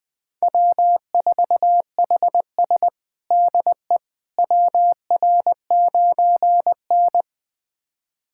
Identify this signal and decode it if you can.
Morse code, heard as a keyed tone, about 20 wpm, 700 Hz. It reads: W4HS DE WR9N
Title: USA Callsign DE Callsign - 4 Characters - 20wpm